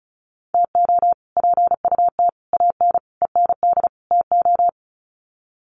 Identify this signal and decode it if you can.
Morse code, heard as a keyed tone, about 34 wpm, 700 Hz. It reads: TO PUT AN END TO
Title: Sets of 5 Words Encore2 35wpm